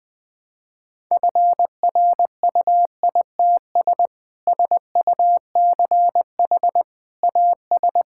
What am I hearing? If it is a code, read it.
Morse code, heard as a keyed tone, about 20 wpm, 700 Hz. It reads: FRUITS SUCH AS
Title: Sentences with 3-Word Phrases - 20wpm